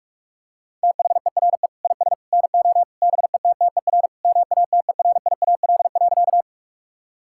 Morse code, heard as a keyed tone, about 46 wpm, 700 Hz. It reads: THERE IS NO BETTER MATERIAL.
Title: Sentences from Top 1000 Words - Review Part 5 - 45wpm